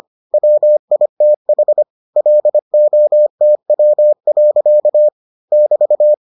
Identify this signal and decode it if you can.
Morse code, heard as a keyed tone, about 25 wpm, 600 Hz. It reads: WITH LOTW. <BT>